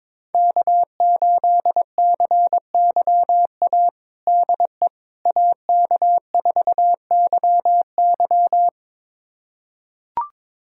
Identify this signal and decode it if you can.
Morse code, heard as a keyed tone, about 22 wpm, 700 Hz. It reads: K8CYA DE AK4YY
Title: USA Callsign DE Callsign - 5 Characters - 22wpm